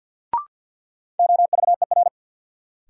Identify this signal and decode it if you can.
Morse code, heard as a keyed tone, about 53 wpm, 700 Hz. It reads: OVER
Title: Sets of 2 Words - New Word Drill - 50wpm